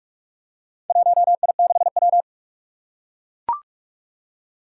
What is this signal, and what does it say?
Morse code, heard as a keyed tone, about 41 wpm, 700 Hz. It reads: 1I6W
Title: Four Challenging Letter-Number Combinations 45wpm